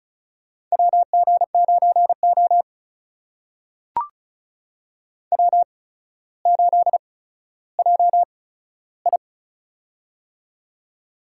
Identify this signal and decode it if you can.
Morse code, heard as a keyed tone, about 36 wpm, spaced as for 15 wpm, 700 Hz. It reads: WG9O W8JI
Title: USA Call Signs - 4 Characters - 8x Character Spacing - 35wpm